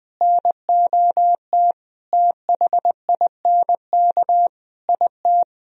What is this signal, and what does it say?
Morse code, heard as a keyed tone, about 20 wpm, 700 Hz. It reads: NOT THINK IT
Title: Speed-Racing - Sets of 5 Words 17wpm